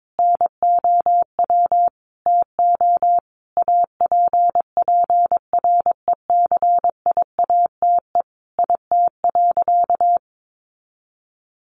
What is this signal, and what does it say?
Morse code, heard as a keyed tone, about 22 wpm, 700 Hz. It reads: NOW TO APPRECIATE IT.
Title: Sentences from Top 1300 Words 22wpm